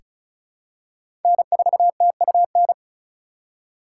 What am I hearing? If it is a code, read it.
Morse code, heard as a keyed tone, about 36 wpm, 690 Hz. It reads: N4TUN
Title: USA Call Signs - 5 Characters - 3x Character Spacing - 35wpm